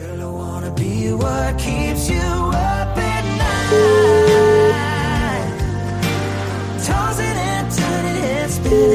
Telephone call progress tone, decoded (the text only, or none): ringback tone